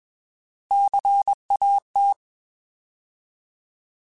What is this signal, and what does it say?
Morse code, heard as a keyed tone, about 21 wpm, 800 Hz. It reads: CAT
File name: Alphabetical / Alphabetical mp3s / Cs @20